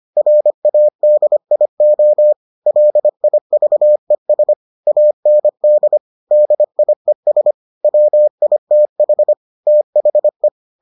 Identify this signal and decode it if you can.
Morse code, heard as a keyed tone, about 25 wpm, 600 Hz. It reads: RADIO LIVES AND DIES WITH THE